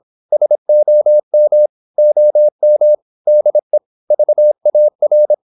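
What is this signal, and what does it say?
Morse code, heard as a keyed tone, about 26 wpm, 600 Hz. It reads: SOM OM DE VAR
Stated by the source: High-speed (QRQ) Morse at 26 wpm.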